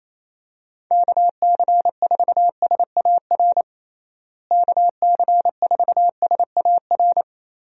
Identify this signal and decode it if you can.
Morse code, heard as a keyed tone, about 28 wpm, 700 Hz. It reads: KC4SAR KC4SAR
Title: USA Call Signs - 6 Characters - Repeated 2x - 28wpm